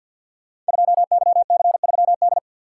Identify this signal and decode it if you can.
Morse code, heard as a keyed tone, about 50 wpm, 690 Hz. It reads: 2YC3D
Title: Code Groups - 5 Letter:Numbers - 50wpm